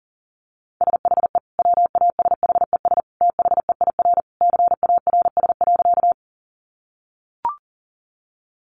Morse code, heard as a keyed tone, about 40 wpm, 700 Hz. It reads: SHE WASHES THEIR CARS.
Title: Easy Phrases 40wpm